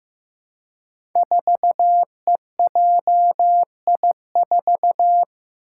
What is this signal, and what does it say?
Morse code, heard as a keyed tone, about 15 wpm, 700 Hz. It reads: 4EJI4
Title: Five Challenging Letter-Number Combinations 15wpm